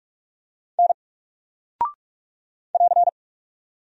Morse code, heard as a keyed tone, about 46 wpm, 700 Hz. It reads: N <AR>
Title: Single Character (Letter:Number:Punctuation:Prosign) 45wpm